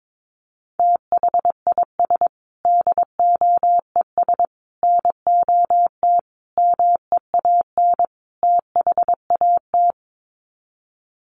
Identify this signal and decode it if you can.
Morse code, heard as a keyed tone, about 22 wpm, 700 Hz. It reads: THIS DOES NOT MEAN THAT
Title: Sets of 5 Words Encore2 22wpm